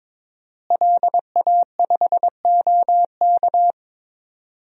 Morse code, heard as a keyed tone, about 22 wpm, 700 Hz. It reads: LA5OK
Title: DXCC-100 Call Signs - 5 Characters - Repeated 3x - 22wpm